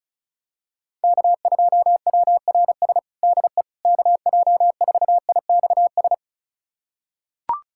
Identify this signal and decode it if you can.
Morse code, heard as a keyed tone, about 36 wpm, 700 Hz. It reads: K2WRS DE KJ4IXS
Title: USA Callsign DE Callsign 35wpm